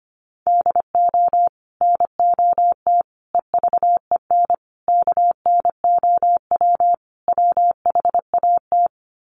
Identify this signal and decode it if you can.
Morse code, heard as a keyed tone, about 25 wpm, 690 Hz. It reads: DO NOT EVEN KNOW WHAT
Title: Sets of 5 Words Encore 25wpm